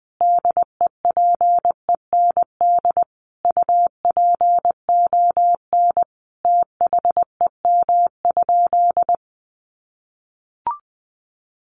Morse code, heard as a keyed tone, about 20 wpm, 700 Hz. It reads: DEPEND UPON THEM?
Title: Sentences from Top 1300 Words 20wpm